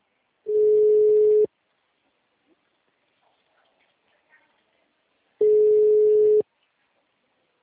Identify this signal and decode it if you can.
Ringback tone (Europe, 425 Hz).